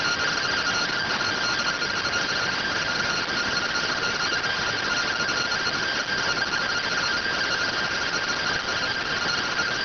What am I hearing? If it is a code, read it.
Two-tone teleprinter signal, about 170 Hz shift.